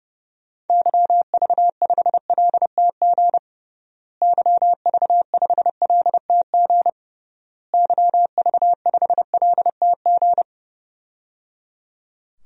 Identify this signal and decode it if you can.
Morse code, heard as a keyed tone, about 30 wpm, 700 Hz. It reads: YV5LTG YV5LTG YV5LTG
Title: DXCC-100 Call Signs - 6 Characters - Repeated 3x - 30wpm